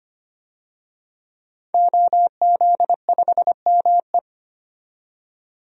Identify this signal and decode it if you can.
Morse code, heard as a keyed tone, about 25 wpm, 700 Hz. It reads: OZ5ME
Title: DXCC-100 Call Signs - 5 Characters - 25wpm